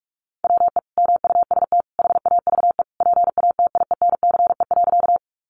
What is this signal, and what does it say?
Morse code, heard as a keyed tone, about 44 wpm, 700 Hz. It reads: WE MUST HAVE PATIENCE.
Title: Sentences from Top 1300 Words 45wpm